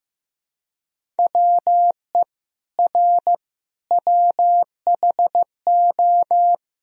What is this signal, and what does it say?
Morse code, heard as a keyed tone, about 15 wpm, 700 Hz. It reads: WE R WHO
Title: Sentences from Top 200 Words and Abbreviations 15wpm